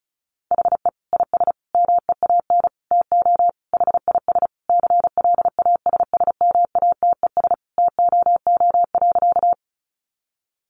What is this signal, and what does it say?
Morse code, heard as a keyed tone, about 34 wpm, 700 Hz. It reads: HE IS MEAN TO HIS CLASSMATES TOO.
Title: Additional Easy Phrases 35wpm